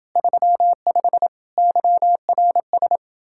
Morse code, heard as a keyed tone, about 30 wpm, 700 Hz. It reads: 35 YRS